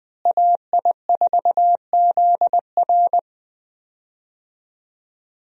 Morse code, heard as a keyed tone, about 20 wpm, 700 Hz. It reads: AI4ZR
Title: USA Call Signs - 5 Characters - Repeated 3x - 20wpm